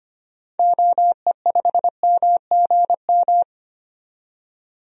Morse code, heard as a keyed tone, about 25 wpm, 690 Hz. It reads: OE5MGM
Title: DXCC-100 Call Signs - 6 Characters - 25wpm